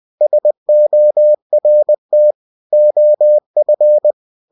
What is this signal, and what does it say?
Morse code, heard as a keyed tone, about 20 wpm, 600 Hz. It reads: SORT OF
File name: cw-books / winnie-the-pooh / winnie-20wpm-page-0103